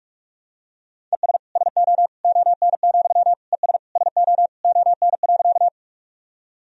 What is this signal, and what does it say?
Morse code, heard as a keyed tone, about 46 wpm, 700 Hz. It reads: ES SO ON, ES SO ON.
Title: Sentences from Top 100 Words and Abbreviations 45wpm